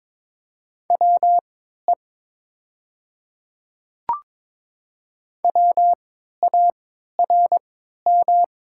Morse code, heard as a keyed tone, about 22 wpm, spaced as for 13 wpm, 700 Hz. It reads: WE WARM
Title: 3x Character Spacing - Top 500 Words - 22wpm